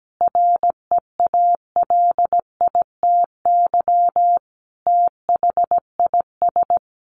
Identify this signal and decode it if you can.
Morse code, heard as a keyed tone, about 17 wpm, 700 Hz. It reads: REALITY THIS
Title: Sentences from Top 1200 Words 17wpm